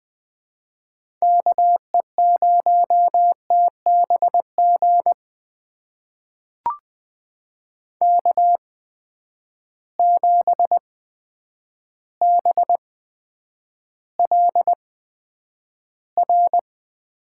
Morse code, heard as a keyed tone, about 20 wpm, spaced as for 9 wpm, 700 Hz. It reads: KE0TBG K7BLR
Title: USA Call Signs - 8x Character Spacing - 20wpm